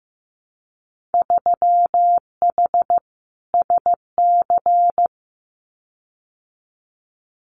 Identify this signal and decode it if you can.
Morse code, heard as a keyed tone, about 15 wpm, 700 Hz. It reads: 3H SC
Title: Winter Field Day Exchanges 15wpm